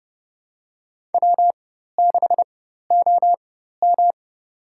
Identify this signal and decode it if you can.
Morse code, heard as a keyed tone, about 31 wpm, spaced as for 14 wpm, 700 Hz. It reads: W6OM
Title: USA Call Signs - 4 Characters - 4x Character Spacing - 30wpm